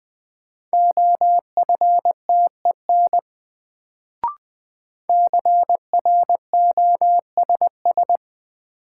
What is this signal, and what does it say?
Morse code, heard as a keyed tone, about 20 wpm, 700 Hz. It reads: OFTEN CROSS
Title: Sentences from Top 300 Words - New Word Drill - 20wpm